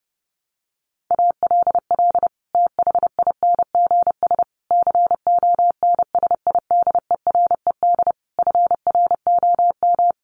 Morse code, heard as a keyed tone, about 30 wpm, 700 Hz. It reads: ALL THINGS CONSIDERED FROM
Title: Sets of 5 Words Encore 30wpm